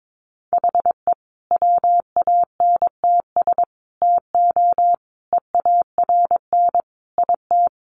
Morse code, heard as a keyed tone, about 22 wpm, 700 Hz. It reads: HE WANTS TO EARN IT
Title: Additional Easy Phrases 22wpm